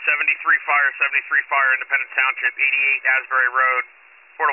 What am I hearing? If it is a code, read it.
Emergency Alert System (SAME) tones, heard as harsh buzzing bursts, with speech. It is heard from 0.4 to 3.5 s.